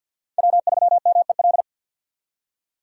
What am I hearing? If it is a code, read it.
Morse code, heard as a keyed tone, about 51 wpm, 690 Hz. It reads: W3MEL
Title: K1USN SST QSO Practice - 50wpm